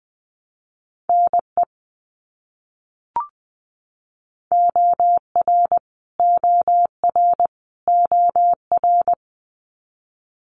Morse code, heard as a keyed tone, about 20 wpm, 700 Hz. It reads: NE OR OR OR
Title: US States Repeated 3x 20wpm